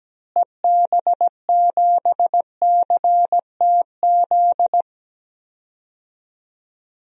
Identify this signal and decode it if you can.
Morse code, heard as a keyed tone, about 17 wpm, 700 Hz. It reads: EB7CTZ
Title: DXCC-100 Call Signs - 6 Characters - Repeated 3x - 17wpm